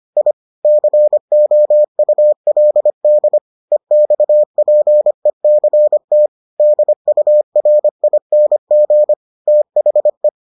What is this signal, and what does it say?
Morse code, heard as a keyed tone, about 25 wpm, 600 Hz. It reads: I COULD EXPECT DURING THE